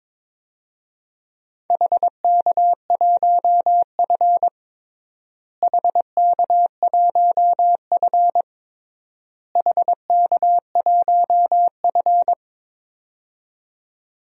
Morse code, heard as a keyed tone, about 22 wpm, 700 Hz. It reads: HK1F HK1F HK1F
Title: DXCC-100 Call Signs - 4 Characters - Repeated 3x - 22wpm